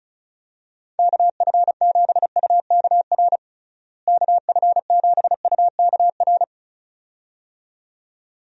Morse code, heard as a keyed tone, about 36 wpm, 700 Hz. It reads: KF7UKR KF7UKR
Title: USA Call Signs - 6 Characters - Repeated 2x - 35wpm